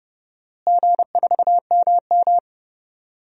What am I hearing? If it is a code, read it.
Morse code, heard as a keyed tone, about 30 wpm, 700 Hz. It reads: G4MM
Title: DXCC-100 Call Signs - 4 Characters - Repeated 2x - 30wpm